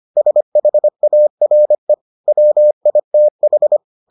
Morse code, heard as a keyed tone, about 25 wpm, 600 Hz. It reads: SHARE WITH